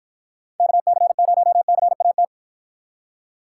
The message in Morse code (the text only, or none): KC0CAT